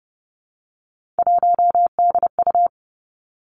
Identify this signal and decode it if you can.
Morse code, heard as a keyed tone, about 30 wpm, 700 Hz. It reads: 1DU
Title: Three Challenging Letter-Number Combinations 30wpm